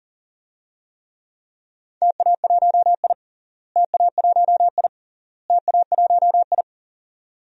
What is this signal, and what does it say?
Morse code, heard as a keyed tone, about 40 wpm, 700 Hz. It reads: TA1I TA1I TA1I
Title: DXCC-100 Call Signs - 4 Characters - Repeated 3x - 40wpm